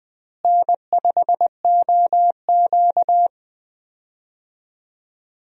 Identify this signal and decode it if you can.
Morse code, heard as a keyed tone, about 20 wpm, 700 Hz. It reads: N5OQ
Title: USA Call Signs - 4 Characters - Repeated 2x - 20wpm